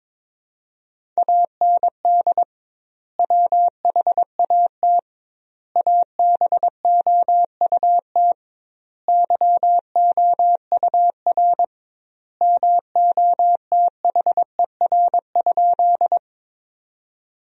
Morse code, heard as a keyed tone, about 22 wpm, 700 Hz. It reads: AND WHAT ABOUT YOUR MOTHER?
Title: 2x Word-Spacing - Sentences from Top 200 Words - 22wpm